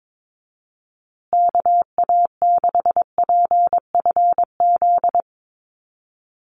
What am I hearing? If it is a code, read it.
Morse code, heard as a keyed tone, about 22 wpm, 700 Hz. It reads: KA6PFZ
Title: USA Call Signs Repeated 2x 22wpm